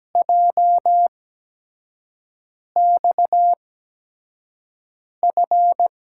Morse code, heard as a keyed tone, about 17 wpm, spaced as for 5 wpm, 700 Hz. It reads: JXF
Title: USA Call Signs - 6 Characters - 8x Character Spacing - 17wpm